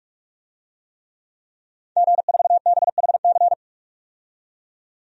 Morse code, heard as a keyed tone, about 45 wpm, 700 Hz. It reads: G4BHC
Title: DXCC-100 Call Signs - 5 Characters - 45wpm